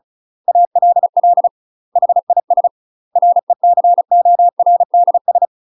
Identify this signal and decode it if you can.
Morse code, heard as a keyed tone, about 34 wpm, 700 Hz. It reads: ALL HIS RECORDS
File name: UK MASTER QSOs TRAINING / 35WPM / UK MASTER QSOs 0036